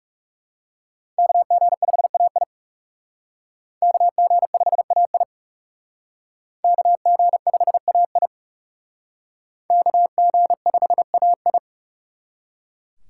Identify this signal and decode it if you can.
Morse code, heard as a keyed tone, about 36 wpm, 700 Hz. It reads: KG5AI KG5AI KG5AI KG5AI
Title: Speed Racing - USA 2x2 Call Signs 30wpm